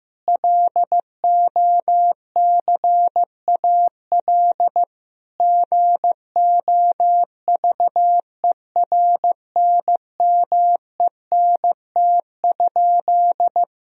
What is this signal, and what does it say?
Morse code, heard as a keyed tone, about 15 wpm, 700 Hz. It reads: LOCAL GOVERNMENT?
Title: Sentences from Top 1000 Words - Review Part 4 - 15wpm